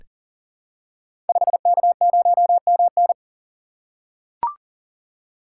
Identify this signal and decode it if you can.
Morse code, heard as a keyed tone, about 38 wpm, 700 Hz. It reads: 5K0MN
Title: Code Groups - 5 Letter:Numbers - 40wpm